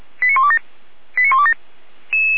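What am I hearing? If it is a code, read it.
Five-tone selective call (ZVEI), heard as quick runs of five tones, tones of about 70 ms. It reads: 89127 89127
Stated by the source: ZVEI five-tone selective call received on SDR.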